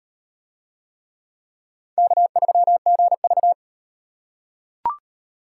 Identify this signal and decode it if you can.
Morse code, heard as a keyed tone, about 38 wpm, 700 Hz. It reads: K3GV